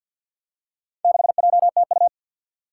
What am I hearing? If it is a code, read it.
Morse code, heard as a keyed tone, about 52 wpm, 700 Hz. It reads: 6JTU
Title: Four Challenging Letter-Number Combinations 50wpm